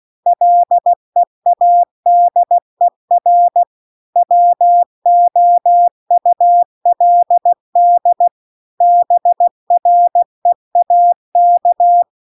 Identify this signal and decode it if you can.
Morse code, heard as a keyed tone, about 16 wpm, 700 Hz. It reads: LEADER WOULD BREAK